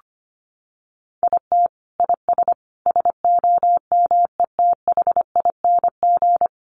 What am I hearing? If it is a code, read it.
Morse code, heard as a keyed tone, about 25 wpm, 700 Hz. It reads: IT IS SOMETHING
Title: Sentences from Top 1100 Words 25wpm